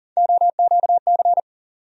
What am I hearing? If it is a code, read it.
Morse code, heard as a keyed tone, about 40 wpm, 700 Hz. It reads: OQC
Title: USA Call Sign Postfixes - 3 Letters 40wpm